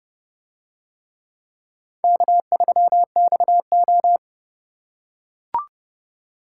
Morse code, heard as a keyed tone, about 29 wpm, 700 Hz. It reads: K3XO